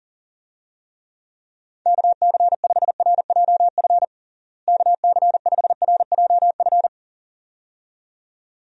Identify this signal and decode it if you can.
Morse code, heard as a keyed tone, about 40 wpm, 700 Hz. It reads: KC5RJF KC5RJF
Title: USA Call Signs - 6 Characters - Repeated 2x - 40wpm